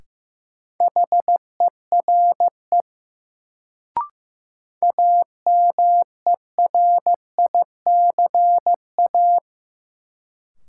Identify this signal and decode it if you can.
Morse code, heard as a keyed tone, about 15 wpm, 700 Hz. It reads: HERE AMERICA
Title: Sets of 5 Words - New Word Drill - 15wpm